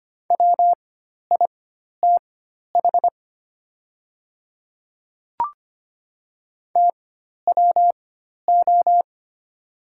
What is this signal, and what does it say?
Morse code, heard as a keyed tone, about 25 wpm, spaced as for 12 wpm, 700 Hz. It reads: WITH TWO
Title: 4x Character Spacing - Top 100 Words - 25wpm